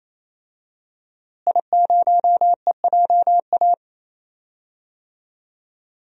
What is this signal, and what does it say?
Morse code, heard as a keyed tone, about 28 wpm, 700 Hz. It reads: I0EJA
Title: DXCC-100 Call Signs - 5 Characters - Repeated 3x - 28wpm